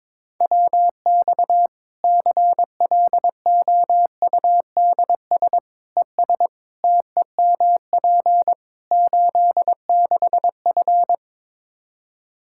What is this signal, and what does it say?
Morse code, heard as a keyed tone, about 22 wpm, 700 Hz. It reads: WX CLOUDS ES TEMP 86F
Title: Practice QSO - Exchange 2 - Info - 22wpm